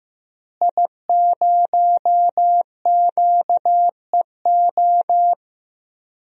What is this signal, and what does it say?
Morse code, heard as a keyed tone, about 15 wpm, 700 Hz. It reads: I0QEO
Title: DXCC-100 Call Signs - 5 Characters - Repeated 3x - 15wpm